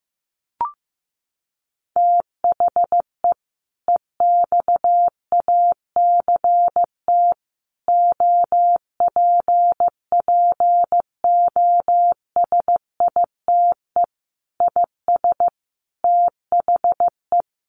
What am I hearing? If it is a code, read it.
Morse code, heard as a keyed tone, about 14 wpm, 700 Hz. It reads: THE EXACT OPPOSITE IS THE
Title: Sentences from Top 1300 Words 15wpm